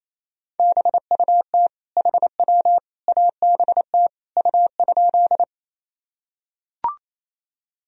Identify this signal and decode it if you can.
Morse code, heard as a keyed tone, about 28 wpm, 700 Hz. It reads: BUT HW ABT U?
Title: Sentences from Top 100 Words and Abbreviations 28wpm